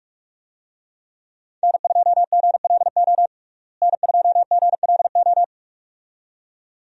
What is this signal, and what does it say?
Morse code, heard as a keyed tone, about 46 wpm, 700 Hz. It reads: N2GLO N2GLO
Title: USA Call Signs - 5 Characters - Repeated 2x - 45wpm